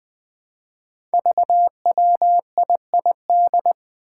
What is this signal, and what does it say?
Morse code, heard as a keyed tone, about 20 wpm, 700 Hz. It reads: VWIID
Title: Five Challenging Letter-Number Combinations 20wpm